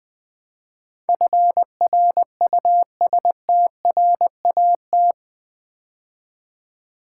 Morse code, heard as a keyed tone, about 20 wpm, 700 Hz. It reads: FRUSTRAT
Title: Build A Word - 4 Letters At A Time - 11-Letter Words - 20wpm